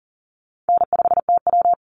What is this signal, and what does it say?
Morse code, heard as a keyed tone, about 40 wpm, 700 Hz. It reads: N5TW
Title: Easy Call Signs - 40wpm